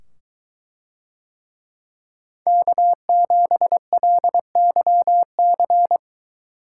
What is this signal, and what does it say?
Morse code, heard as a keyed tone, about 23 wpm, 700 Hz. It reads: K7LYC